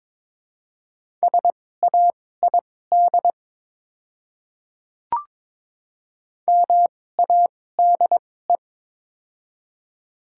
Morse code, heard as a keyed tone, about 22 wpm, spaced as for 16 wpm, 700 Hz. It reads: SAID MADE
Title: 2x Character Spacing - Top 300 Words - 22wpm